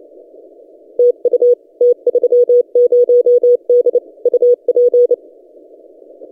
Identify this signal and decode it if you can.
Morse code, heard as a keyed tone, about 28 wpm, 490 Hz. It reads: TU T30D UP
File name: t30d-12cw